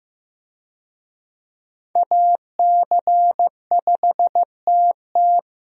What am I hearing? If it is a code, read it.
Morse code, heard as a keyed tone, about 15 wpm, 700 Hz. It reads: AC5TT
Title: USA Call Signs - 5 Characters - Repeated 3x - 15wpm